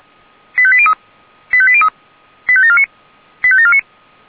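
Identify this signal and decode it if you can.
Five-tone selective call (ZVEI), heard as quick runs of five tones, tones of about 70 ms. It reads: 85803 85803 85749 85749
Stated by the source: ZVEI five-tone selective call received on SDR.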